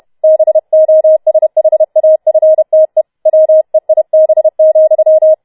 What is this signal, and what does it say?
Morse code, heard as a keyed tone, about 31 wpm, 630 Hz. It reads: BOSHAFTE WEIB,
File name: Schneewittchen Teil4 32wpm